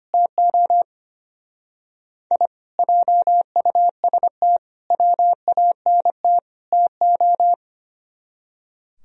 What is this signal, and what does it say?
Morse code, heard as a keyed tone, about 29 wpm, 700 Hz. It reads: TO I JUST WANT TO
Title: Speed-Racing - Sets of 4 Words 25wpm